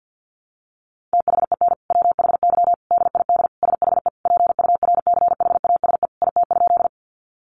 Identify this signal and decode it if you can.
Morse code, heard as a keyed tone, about 50 wpm, 700 Hz. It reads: THEN WHY DID SHE PURCHASE IT?
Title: Sentences from Top 800 Words 50wpm